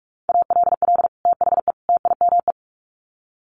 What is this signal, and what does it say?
Morse code, heard as a keyed tone, about 44 wpm, 700 Hz. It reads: ALL THE TIME
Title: Sets of 3 Words 45wpm